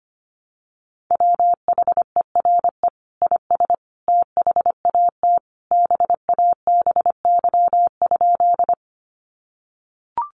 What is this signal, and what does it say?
Morse code, heard as a keyed tone, about 25 wpm, 700 Hz. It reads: WHERE IS THAT BABY?
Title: Sentences from Top 1300 Words 25wpm